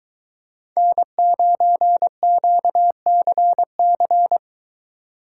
Morse code, heard as a keyed tone, about 22 wpm, 700 Hz. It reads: N9QCC